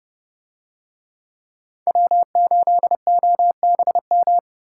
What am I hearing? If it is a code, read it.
Morse code, heard as a keyed tone, about 29 wpm, 700 Hz. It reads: W8OBM